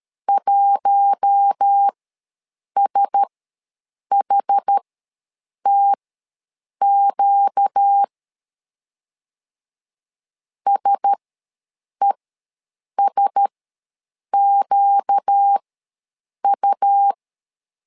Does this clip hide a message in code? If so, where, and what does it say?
Yes: Morse code from 0.3 to 17.1 s, heard as a keyed tone, about 13 wpm, spaced as for 7 wpm, 800 Hz. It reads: 1SHTQ SESQU